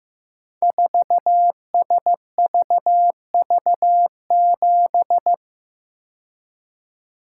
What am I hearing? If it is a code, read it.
Morse code, heard as a keyed tone, about 15 wpm, 700 Hz. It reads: 4SVV7
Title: Code Groups - 5 Letter:Numbers - 15wpm